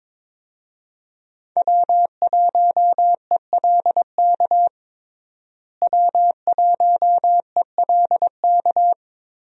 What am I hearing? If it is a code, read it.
Morse code, heard as a keyed tone, about 22 wpm, 700 Hz. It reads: W1ELK W1ELK
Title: USA Call Signs - 5 Characters - Repeated 3x - 22wpm